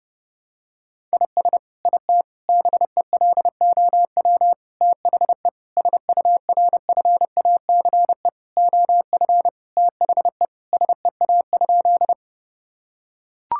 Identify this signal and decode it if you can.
Morse code, heard as a keyed tone, about 30 wpm, 700 Hz. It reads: IS IT BELOW THE SURFACE OF THE SEA?
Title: Sentences from Top 1000 Words - Review Part 5 - 30wpm